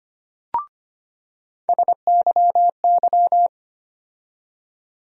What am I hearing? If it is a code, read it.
Morse code, heard as a keyed tone, about 25 wpm, 700 Hz. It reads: SYY
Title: USA Call Sign Postfixes - 3 Letters 25wpm